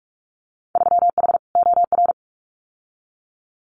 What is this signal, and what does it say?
Morse code, heard as a keyed tone, about 46 wpm, 690 Hz. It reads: 3H OR
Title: Winter Field Day Exchanges Encore 45wpm